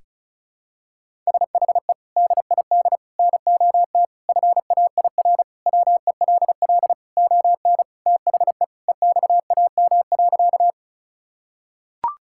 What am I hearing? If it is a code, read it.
Morse code, heard as a keyed tone, about 36 wpm, 700 Hz. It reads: SHE DID NOT FAIR WELL ON THE EXAM.
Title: Sentences from Top 1000 Words - Review Part 5 - 35wpm